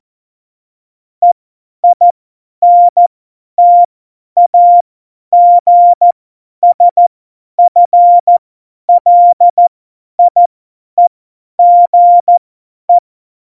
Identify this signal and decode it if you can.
Morse code, heard as a keyed tone, about 14 wpm, spaced as for 10 wpm, 700 Hz. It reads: EINTAGSFLIEGE
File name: Morsekurs / Witze / Witz 24 April 2018, 50 F70